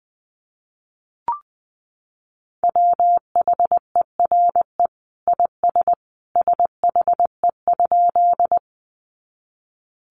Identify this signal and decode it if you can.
Morse code, heard as a keyed tone, about 20 wpm, 700 Hz. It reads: WHERE IS SHE?
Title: Sentences from Top 200 Words 20wpm